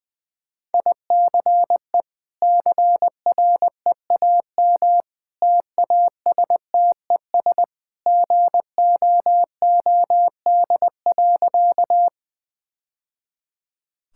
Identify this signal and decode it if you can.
Morse code, heard as a keyed tone, about 20 wpm, 700 Hz. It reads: ICE CREAM TASTES GOOD.
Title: Easy Sentences with Hints - 20wpm